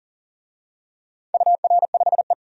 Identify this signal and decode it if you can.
Morse code, heard as a keyed tone, about 38 wpm, 700 Hz. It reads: UR5E
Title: DXCC-100 Call Signs - 4 Characters - 40wpm